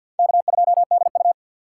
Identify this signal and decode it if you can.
Morse code, heard as a keyed tone, about 63 wpm, 700 Hz. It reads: K2DU